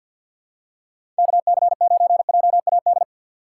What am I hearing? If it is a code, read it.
Morse code, heard as a keyed tone, about 50 wpm, 700 Hz. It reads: KC9JAD
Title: USA Call Signs - 6 Characters - 6x Character Spacing - 50wpm